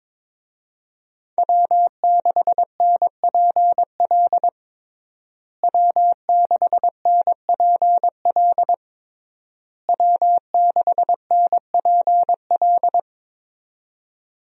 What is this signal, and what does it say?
Morse code, heard as a keyed tone, about 22 wpm, 700 Hz. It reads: W6NPL W6NPL W6NPL
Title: USA Call Signs - 5 Characters - Repeated 3x - 22wpm